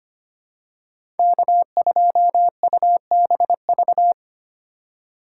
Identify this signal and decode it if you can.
Morse code, heard as a keyed tone, about 25 wpm, 700 Hz. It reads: K2UBV
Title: USA Call Signs - 5 Characters - 4x Character Spacing - 25wpm